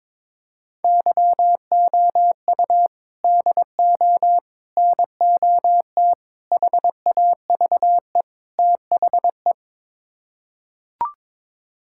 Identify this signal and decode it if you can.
Morse code, heard as a keyed tone, about 22 wpm, 700 Hz. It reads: YOU DO NOT HAVE THE
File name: practice / 22wpm / Build A Phrase - 5-Word Phrases - 22wpm